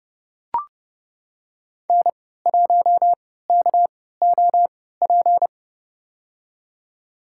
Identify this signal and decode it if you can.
Morse code, heard as a keyed tone, about 30 wpm, spaced as for 17 wpm, 700 Hz. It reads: N1KOP
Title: USA Call Signs - 5 Characters - 3x Character Spacing - 30wpm